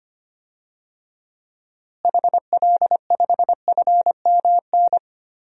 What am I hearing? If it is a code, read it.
Morse code, heard as a keyed tone, about 25 wpm, 700 Hz. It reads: HL5FMN
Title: DXCC-100 Call Signs - 6 Characters - Repeated 2x - 25wpm